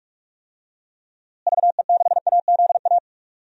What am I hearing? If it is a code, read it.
Morse code, heard as a keyed tone, about 44 wpm, 700 Hz. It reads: VE6AZA
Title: DXCC-100 Call Signs - 6 Characters - Repeated 3x - 45wpm